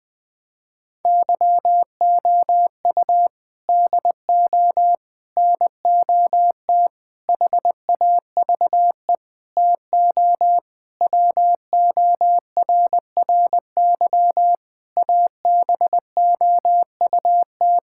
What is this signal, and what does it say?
Morse code, heard as a keyed tone, about 20 wpm, 700 Hz. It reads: YOU DO NOT HAVE TO WORRY ABOUT
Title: Sentences with 5-Word Phrases - Spoken Beginning - 20wpm